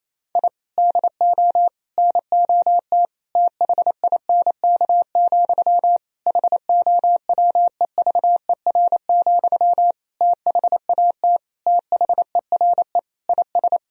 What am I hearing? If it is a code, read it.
Morse code, heard as a keyed tone, about 28 wpm, 700 Hz. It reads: I DO NOT THINK, HOWEVER, THAT THERE IS
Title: Sentences from Top 1000 Words - Review Part 3 - 28wpm